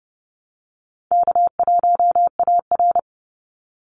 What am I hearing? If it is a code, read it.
Morse code, heard as a keyed tone, about 30 wpm, 700 Hz. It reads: K1AR
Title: Easy Call Signs Repeated 3x - 30wpm